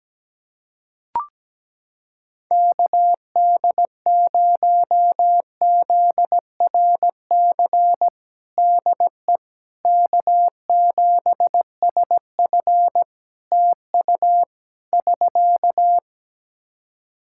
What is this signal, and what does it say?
Morse code, heard as a keyed tone, about 17 wpm, 700 Hz. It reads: KD0ZRC DE K7SF TU <SK>
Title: Practice QSO - Exchange 4 - Ending - 17wpm